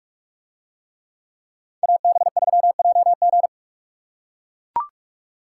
Morse code, heard as a keyed tone, about 44 wpm, 700 Hz. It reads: AB3JG